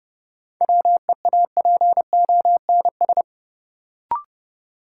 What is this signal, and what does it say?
Morse code, heard as a keyed tone, about 30 wpm, 700 Hz. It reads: WEAPONS
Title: Sets of 5 Words Encore - New Word Drill - 30wpm